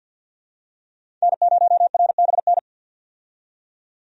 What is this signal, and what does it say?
Morse code, heard as a keyed tone, about 48 wpm, 700 Hz. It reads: N0RBN
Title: USA Call Signs - 5 Characters - 50wpm